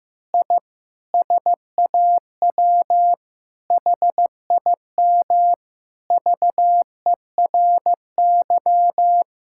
Morse code, heard as a keyed tone, about 15 wpm, 700 Hz. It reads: I SAW HIM VERY
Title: Sentences from Top 1000 Words - Review Part 5 - 15wpm